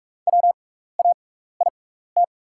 Morse code, heard as a keyed tone, about 45 wpm, spaced as for 15 wpm, 700 Hz. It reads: WAIT
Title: 6x Character Spacing - Top 500 Words - 45wpm